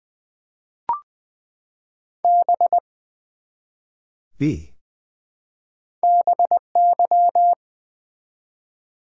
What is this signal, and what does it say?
Morse code, heard as a keyed tone, about 20 wpm, 700 Hz. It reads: B BY
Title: Build A Word - 1 Letter At A Time - 3-Letter Words - 20wpm